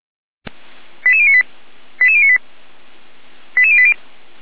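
Five-tone selective call (ZVEI), heard as quick runs of five tones, tones of about 70 ms. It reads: 88098 88098 88080
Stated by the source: ZVEI five-tone call received on SDR.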